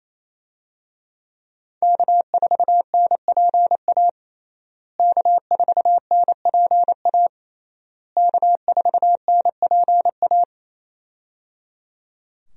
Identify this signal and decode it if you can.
Morse code, heard as a keyed tone, about 28 wpm, 700 Hz. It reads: K4NPA K4NPA K4NPA
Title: USA Call Signs - 5 Characters - Repeated 3x - 28wpm